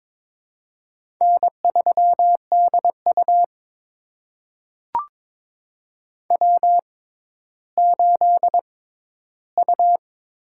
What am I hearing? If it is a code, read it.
Morse code, heard as a keyed tone, about 23 wpm, spaced as for 13 wpm, 700 Hz. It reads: N3DU W8U
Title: USA Call Signs - 4 Characters - 6x Character Spacing - 22wpm